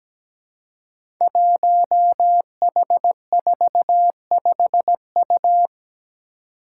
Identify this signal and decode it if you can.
Morse code, heard as a keyed tone, about 17 wpm, 700 Hz. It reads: 1H45U
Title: Five Challenging Letter-Number Combinations 17wpm